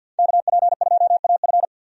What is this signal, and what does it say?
Morse code, heard as a keyed tone, about 46 wpm, 700 Hz. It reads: KP2AF
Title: USA Call Signs - 5 Characters - 50wpm